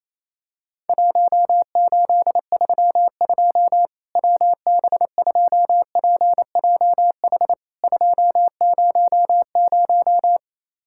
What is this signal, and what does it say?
Morse code, heard as a keyed tone, about 28 wpm, 700 Hz. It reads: 1832 WB2PJH 200
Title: Worked All Europe DX Contest - QTC Exchange - 28wpm